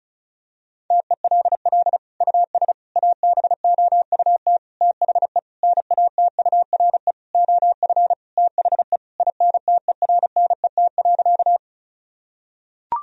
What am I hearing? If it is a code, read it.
Morse code, heard as a keyed tone, about 36 wpm, 700 Hz. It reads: TELL US ABOUT THE NATURE OF THE INTERNET.
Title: Sentences from Top 1000 Words - Review Part 4 - 35wpm